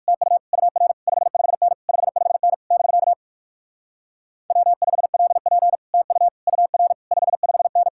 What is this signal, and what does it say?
Morse code, heard as a keyed tone, about 61 wpm, 700 Hz. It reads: TU UR 55N 55N <BK> W5LP TU UR 55N